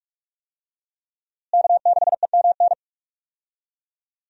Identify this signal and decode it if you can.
Morse code, heard as a keyed tone, about 42 wpm, 700 Hz. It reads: K6EMN
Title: USA Call Signs - 5 Characters - Repeated 2x - 45wpm